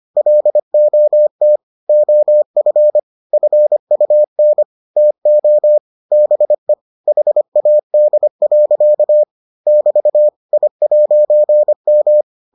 Morse code, heard as a keyed tone, about 25 wpm, 600 Hz. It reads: LOT OF FUN TO BE HAD. <BT> I'M